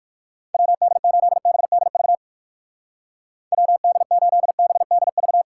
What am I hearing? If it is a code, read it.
Morse code, heard as a keyed tone, about 57 wpm, 700 Hz. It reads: WD8BDV WD8BDV